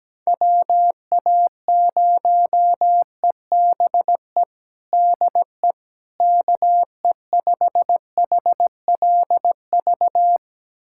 Morse code, heard as a keyed tone, about 17 wpm, 700 Hz. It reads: WA0EBE DE KE5HLV
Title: USA Callsign DE Callsign - 6 Characters - 17wpm